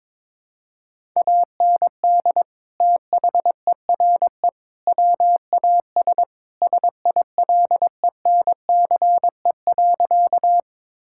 Morse code, heard as a keyed tone, about 21 wpm, 700 Hz. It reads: AND THERE WAS SILENCE.
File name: practice / 22wpm / Sentences with 3-Word Phrases - 22wpm